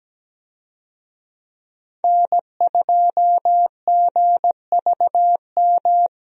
Morse code, heard as a keyed tone, about 17 wpm, 700 Hz. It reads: N2GVM
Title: USA Call Signs - 5 Characters - Repeated 2x - 17wpm